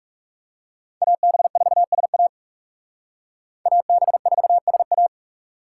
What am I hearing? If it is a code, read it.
Morse code, heard as a keyed tone, about 36 wpm, 700 Hz. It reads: AB4SA AB4SA
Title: Speed Racing - USA Call Signs 30wpm